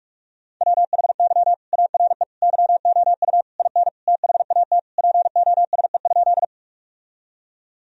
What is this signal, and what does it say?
Morse code, heard as a keyed tone, about 41 wpm, 700 Hz. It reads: WHY ARE YOU IN THAT POSE?
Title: Sentences from Top 1000 Words - Review Part 2 - 45wpm